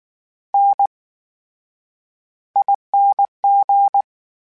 Morse code, heard as a keyed tone, about 19 wpm, spaced as for 9 wpm, 800 Hz. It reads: NING
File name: Special INGs / Special INGs-mp3s / VerySlowToFaster Ing @18